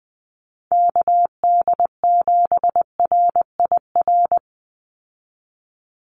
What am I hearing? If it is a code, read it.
Morse code, heard as a keyed tone, about 19 wpm, 700 Hz. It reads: KD7RIR
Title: USA Call Signs 20wpm